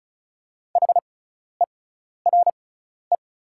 Morse code, heard as a keyed tone, about 36 wpm, spaced as for 12 wpm, 700 Hz. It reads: HERE
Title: 6x Character Spacing - Top 200 Words - 35wpm